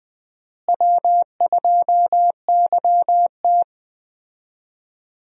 Morse code, heard as a keyed tone, about 20 wpm, 700 Hz. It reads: W2YT
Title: USA Call Signs - 4 Characters - Repeated 3x - 20wpm